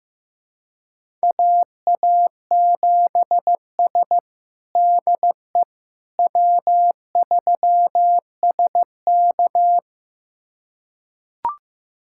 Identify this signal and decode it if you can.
Morse code, heard as a keyed tone, about 16 wpm, 700 Hz. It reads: AA7S DE W3SK
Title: USA Callsign DE Callsign - 4 Characters - 15wpm